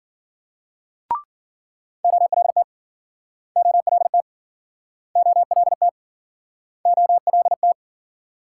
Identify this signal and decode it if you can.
Morse code, heard as a keyed tone, about 54 wpm, 700 Hz. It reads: OLT OLT OLT OLT